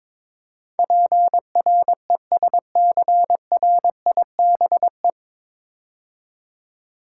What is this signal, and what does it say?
Morse code, heard as a keyed tone, about 22 wpm, 700 Hz. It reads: PRESCRIBE
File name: 9 Letter Words - First 3 Letters Given - 22wpm